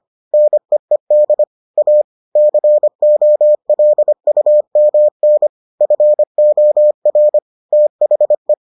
Morse code, heard as a keyed tone, about 25 wpm, 600 Hz. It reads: NEED A COLUMN FOR THE